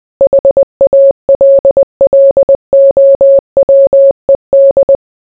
Morse code, heard as a keyed tone, about 20 wpm, 550 Hz. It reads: HALLOWED